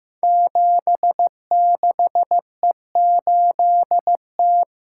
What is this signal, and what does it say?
Morse code, heard as a keyed tone, about 15 wpm, 700 Hz. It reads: 76E8T
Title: Code Groups - 5 Letter:Numbers - 15wpm